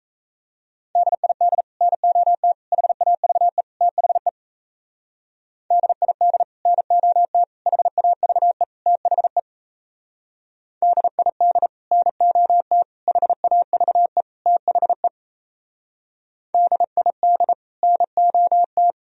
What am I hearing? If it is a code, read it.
Morse code, heard as a keyed tone, about 36 wpm, 700 Hz. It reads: DID NOT HAVE THE DID NOT HAVE THE DID NOT HAVE THE DID NOT